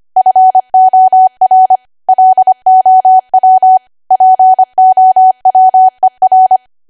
Morse code, heard as a keyed tone, about 25 wpm, 750 Hz. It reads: FOR LOW POWER
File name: Archive / 25 WPM / 25WPM